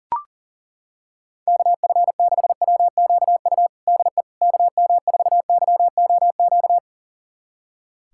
Morse code, heard as a keyed tone, about 40 wpm, 700 Hz. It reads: KF6WQU DE KM4YOQ
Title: USA Callsign DE Callsign - 6 Characters - 40wpm